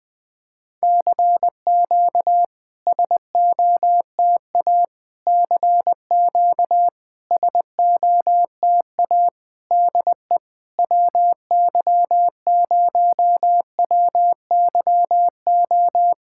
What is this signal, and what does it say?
Morse code, heard as a keyed tone, about 20 wpm, 700 Hz. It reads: CQ SOTA CQ SOTA DE WY0WYO
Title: SOTA QSO Practice 20wpm